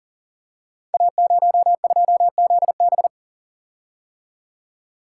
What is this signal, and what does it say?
Morse code, heard as a keyed tone, about 42 wpm, 690 Hz. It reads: A02ZB
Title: Code Groups - 5 Letter:Numbers - 40wpm